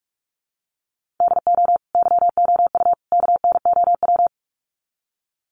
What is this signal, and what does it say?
Morse code, heard as a keyed tone, about 44 wpm, 700 Hz. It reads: DO YOU KNOW
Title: Sets of 3 Words 45wpm